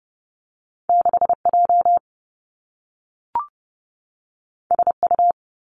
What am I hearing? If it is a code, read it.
Morse code, heard as a keyed tone, about 30 wpm, 700 Hz. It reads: 6 J S U
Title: Two Challenging Letter-Number Combinations 30wpm